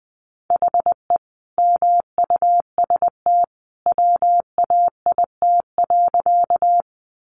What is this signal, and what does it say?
Morse code, heard as a keyed tone, about 20 wpm, 700 Hz. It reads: HE MUST WAIT.
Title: Sentences from Top 500 Words 20wpm